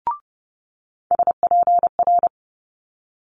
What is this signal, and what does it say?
Morse code, heard as a keyed tone, about 30 wpm, 700 Hz. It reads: SPR
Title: Multi-letter Phonemes Quick Fire 30wpm